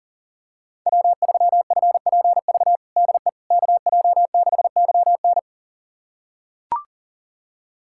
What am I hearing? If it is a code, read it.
Morse code, heard as a keyed tone, about 37 wpm, 700 Hz. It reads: W3FPV DE KJ6YN
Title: USA Callsign DE Callsign - 5 Characters - 40wpm